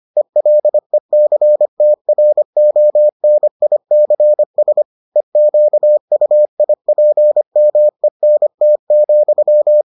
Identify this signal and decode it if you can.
Morse code, heard as a keyed tone, about 25 wpm, 600 Hz. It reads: ELECTRONICS EQUIPMENT,